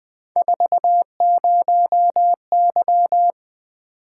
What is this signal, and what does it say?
Morse code, heard as a keyed tone, about 20 wpm, 700 Hz. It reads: 40Y
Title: Three Letter-Number Combinations - 20wpm